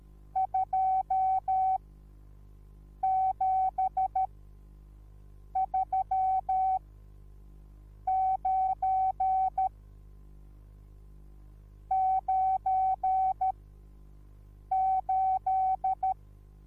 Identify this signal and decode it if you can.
Morse code, heard as a keyed tone, about 13 wpm, 760 Hz. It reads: 2 7 3 9 9 8
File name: Onlinelehrgang / morsen / DJ4UF Morsen21